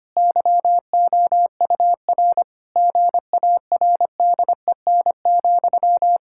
Morse code, heard as a keyed tone, about 25 wpm, 700 Hz. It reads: YOUR GARDEN,
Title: Pangram Challenge v2 - 25wpm